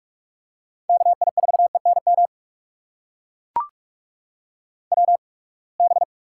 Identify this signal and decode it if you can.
Morse code, heard as a keyed tone, about 44 wpm, spaced as for 31 wpm, 700 Hz. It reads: KI4ENM WB
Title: USA Call Signs - 6 Characters - 8x Character Spacing - 45wpm